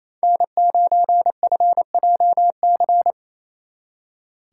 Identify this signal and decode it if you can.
Morse code, heard as a keyed tone, about 37 wpm, 700 Hz. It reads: N9FJC